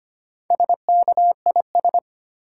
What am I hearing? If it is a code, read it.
Morse code, heard as a keyed tone, about 25 wpm, 690 Hz. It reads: SKIS
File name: practice / 25wpm / Easy Words with Hints - 25wpm